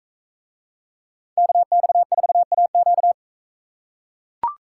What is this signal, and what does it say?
Morse code, heard as a keyed tone, about 42 wpm, 700 Hz. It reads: KX4AQ